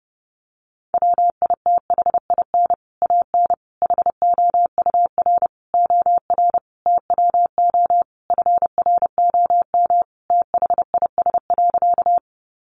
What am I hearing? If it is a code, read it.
Morse code, heard as a keyed tone, about 30 wpm, 700 Hz. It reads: WITHIN AN HOUR OR TWO FROM THIS.
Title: Sentences from Top 700 Words 30wpm